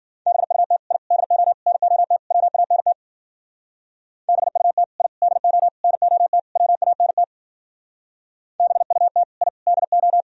Morse code, heard as a keyed tone, about 50 wpm, 700 Hz. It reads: BUT I DO NOT WANT BUT I DO NOT WANT BUT I DO
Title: Speed-Racing - Sets of 5 Words 40wpm